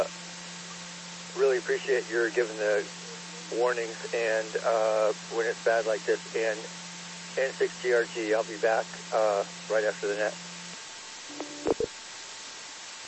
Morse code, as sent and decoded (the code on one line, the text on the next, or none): none